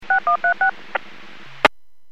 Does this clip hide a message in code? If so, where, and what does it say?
Yes: DTMF from 0.1 to 0.7 s, heard as telephone touch tones, tones of about 110 ms. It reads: 6136